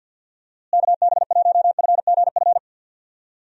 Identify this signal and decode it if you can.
Morse code, heard as a keyed tone, about 50 wpm, 700 Hz. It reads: KB1FGF
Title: USA Call Signs - 6 Characters - Repeated 2x - 50wpm